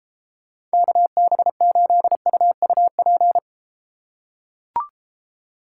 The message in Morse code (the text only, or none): KB8UUP